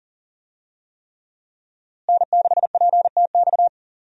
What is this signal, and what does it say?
Morse code, heard as a keyed tone, about 40 wpm, 700 Hz. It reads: N6PTX
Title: USA Call Signs - 5 Characters - Repeated 3x - 40wpm